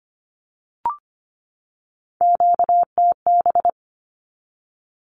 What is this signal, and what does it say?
Morse code, heard as a keyed tone, about 25 wpm, 700 Hz. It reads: QTB
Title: Q-Codes 25wpm